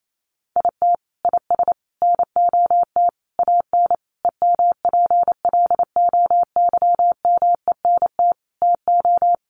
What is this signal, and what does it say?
Morse code, heard as a keyed tone, about 28 wpm, 700 Hz. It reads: IT IS NOT AN EMPLOYMENT TO
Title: Sentences from Top 1500 Words 28wpm